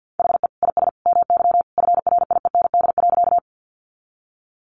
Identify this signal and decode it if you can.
Morse code, heard as a keyed tone, about 50 wpm, 700 Hz. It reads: HE IS MY FRIEND.
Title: Sentences from Top 300 Words 50wpm